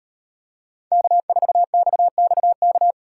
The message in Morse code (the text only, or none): K4XXK